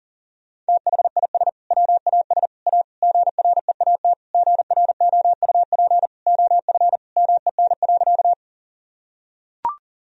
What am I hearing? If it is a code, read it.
Morse code, heard as a keyed tone, about 37 wpm, 700 Hz. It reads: THIS WAS A GREAT GROUP OF MEN.
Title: Sentences from Top 1000 Words - Review Part 1 - 40wpm